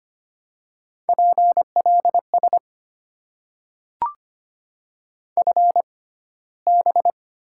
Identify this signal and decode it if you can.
Morse code, heard as a keyed tone, about 25 wpm, spaced as for 15 wpm, 700 Hz. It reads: PLS FB
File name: Top 90 QSO Elements - 6x Character Spacing - 25wpm